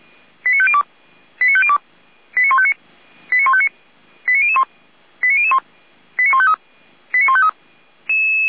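Five-tone selective call (ZVEI), heard as quick runs of five tones, tones of about 70 ms. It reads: 89502 89502 89159 89159 89001 89001 89153 89153
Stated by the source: ZVEI five-tone selective call 89150, received by SDR.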